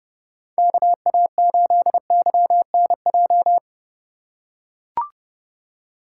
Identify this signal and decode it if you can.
Morse code, heard as a keyed tone, about 30 wpm, 700 Hz. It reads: KA8YNJ
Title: USA Call Signs - 6 Characters - 3x Character Spacing - 30wpm